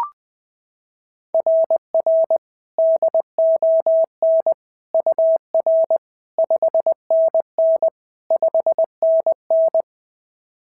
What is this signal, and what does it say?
Morse code, heard as a keyed tone, about 20 wpm, 690 Hz. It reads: RR DON UR 5NN 5NN
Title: SOTA QSO Practice 20wpm